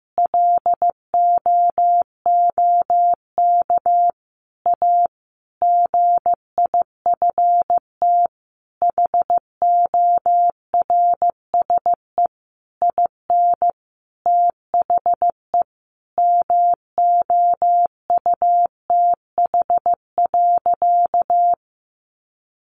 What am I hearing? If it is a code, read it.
Morse code, heard as a keyed tone, about 15 wpm, 700 Hz. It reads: LOOK A GIFT HORSE IN THE MOUTH.
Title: Sentences from Top 1500 Words 15wpm